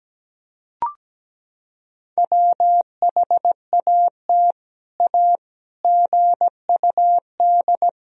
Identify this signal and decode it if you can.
Morse code, heard as a keyed tone, about 17 wpm, 700 Hz. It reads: WHAT A GUD
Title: Sentences from Top 200 Words and Abbreviations 17wpm